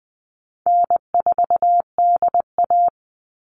Morse code, heard as a keyed tone, about 20 wpm, 700 Hz. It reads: N4DA
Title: NA SOTA Chaser Callsigns 20wpm